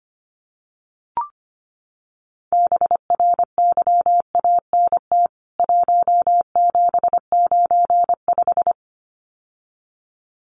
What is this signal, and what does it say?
Morse code, heard as a keyed tone, about 25 wpm, 700 Hz. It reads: BRYANT 1795
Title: CWT Exchange 25wpm